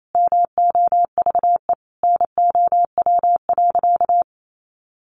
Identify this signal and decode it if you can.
Morse code, heard as a keyed tone, about 28 wpm, 700 Hz. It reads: MOVE NOW.
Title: Sentences from Top 1200 Words 28wpm